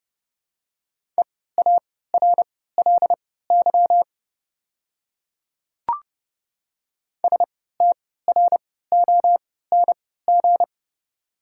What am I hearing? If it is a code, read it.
Morse code, heard as a keyed tone, about 30 wpm, spaced as for 17 wpm, 690 Hz. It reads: EARLY STRONG
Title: 3x Character Spacing - Top 500 Words - 30wpm